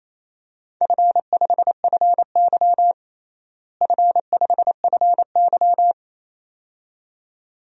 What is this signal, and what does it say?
Morse code, heard as a keyed tone, about 28 wpm, 700 Hz. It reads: F5FY F5FY
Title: DXCC-100 Call Signs - 4 Characters - Repeated 2x - 28wpm